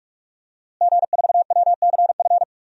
Morse code, heard as a keyed tone, about 44 wpm, 700 Hz. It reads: G4WCF
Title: DXCC-100 Call Signs - 5 Characters - Repeated 3x - 45wpm